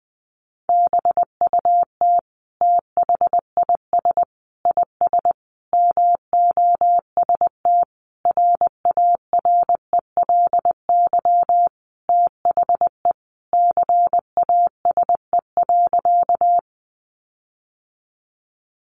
Morse code, heard as a keyed tone, about 20 wpm, 700 Hz. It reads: BUT THIS IS MOST RARELY THE CASE.
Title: Sentences from Top 1400 Words 20wpm